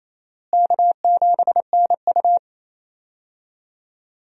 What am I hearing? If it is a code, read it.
Morse code, heard as a keyed tone, about 28 wpm, 700 Hz. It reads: K7NU
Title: USA Call Signs - 4 Characters - Repeated 3x - 28wpm